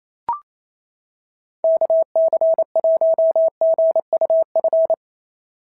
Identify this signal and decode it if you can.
Morse code, heard as a keyed tone, about 27 wpm, 650 Hz. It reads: KC1GUF
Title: SOTA QSO Practice 28wpm